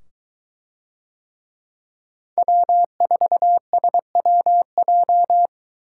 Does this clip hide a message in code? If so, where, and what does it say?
Yes: Morse code from 2.4 to 5.5 s, heard as a keyed tone, about 23 wpm, 700 Hz. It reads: W4SWJ